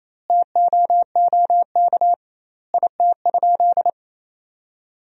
Morse code, heard as a keyed tone, about 28 wpm, 700 Hz. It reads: TOOK IT?
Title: 2x Word-Spacing - Sentences from Top 300 Words - 28wpm